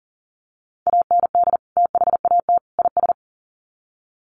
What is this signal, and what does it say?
Morse code, heard as a keyed tone, about 40 wpm, 700 Hz. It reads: AND THAT IS
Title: Sets of 3 Words 40wpm